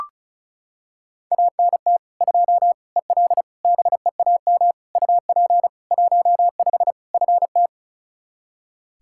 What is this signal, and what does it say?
Morse code, heard as a keyed tone, about 34 wpm, 700 Hz. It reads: ANT 2 EL BEAM UP 15 FT
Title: Practice QSO - Exchange 2 - Info - 35wpm